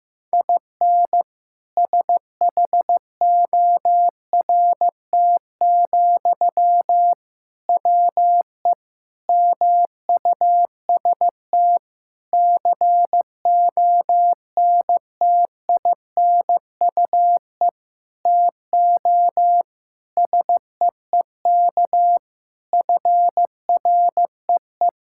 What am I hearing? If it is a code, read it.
Morse code, heard as a keyed tone, about 16 wpm, 700 Hz. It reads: IN SHORT, WE MUST CONTINUE TO SEEK FREE
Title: Sentences from Top 1600 Words 15wpm